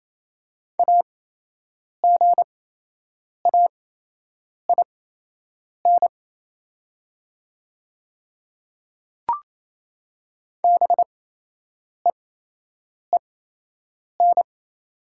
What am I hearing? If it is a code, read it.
Morse code, heard as a keyed tone, about 28 wpm, spaced as for 8 wpm, 700 Hz. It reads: AGAIN BEEN
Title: 8x Character Spacing - Top 500 Words - 28wpm